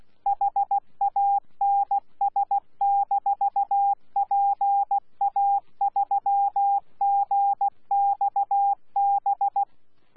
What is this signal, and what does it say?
Morse code, heard as a keyed tone, about 16 wpm, 800 Hz. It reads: HANS-PA3GXB
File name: Z PI4VRZ / PI4VRZ 14-jun-2025